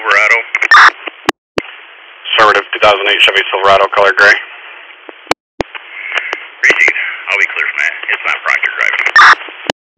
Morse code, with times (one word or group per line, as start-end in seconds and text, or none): none